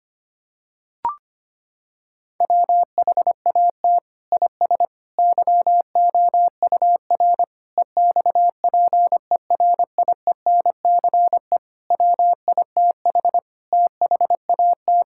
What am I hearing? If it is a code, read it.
Morse code, heard as a keyed tone, about 25 wpm, 700 Hz. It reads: WHAT IS YOUR EXPERIENCE WITH THAT
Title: Sentences from Top 1000 Words - Review Part 5 - 25wpm